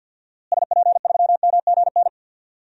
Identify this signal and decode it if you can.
Morse code, heard as a keyed tone, about 52 wpm, 700 Hz. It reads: SP3MGN
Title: DXCC-100 Call Signs - 6 Characters - Repeated 3x - 50wpm